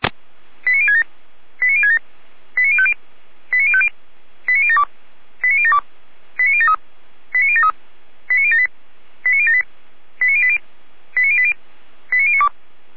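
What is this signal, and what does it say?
Five-tone selective call (ZVEI), heard as quick runs of five tones, tones of about 70 ms. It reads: 89067 89067 89050 89050 89072 89072 89073 89073 89078 89078 89080 89080 89092
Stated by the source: ZVEI five-tone selective call 89001 received by SDR, archived by Feuerwehr Orlamünde (fire brigade).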